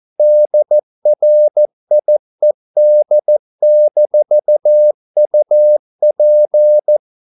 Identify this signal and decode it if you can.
Morse code, heard as a keyed tone, about 14 wpm, 600 Hz. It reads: DRIED-UP